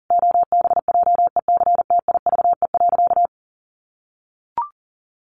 Morse code, heard as a keyed tone, about 40 wpm, 700 Hz. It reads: OBJECTIVE.
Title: Sentences from Top 1200 Words 40wpm